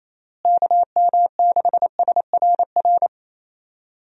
Morse code, heard as a keyed tone, about 28 wpm, 700 Hz. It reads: KM6SRR
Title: USA Call Signs - 6 Characters - Repeated 3x - 28wpm